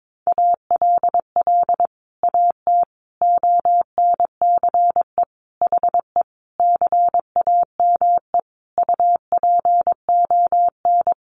Morse code, heard as a keyed tone, about 22 wpm, 700 Hz. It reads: ALL AT ONCE HE CAME UPON
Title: Sentences from Top 1100 Words 22wpm